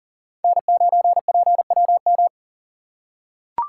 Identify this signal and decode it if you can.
Morse code, heard as a keyed tone, about 40 wpm, 700 Hz. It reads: N9PWM
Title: USA Call Signs - 6x Character Spacing - 40wpm